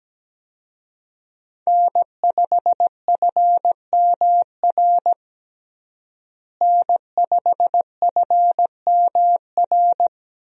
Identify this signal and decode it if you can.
Morse code, heard as a keyed tone, about 17 wpm, 700 Hz. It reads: N5FMR N5FMR
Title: USA Call Signs - 5 Characters - Repeated 3x - 17wpm